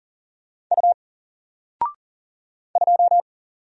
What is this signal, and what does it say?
Morse code, heard as a keyed tone, about 39 wpm, 700 Hz. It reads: U 2
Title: Single Character (Letter:Number:Punctuation:Prosign) 40wpm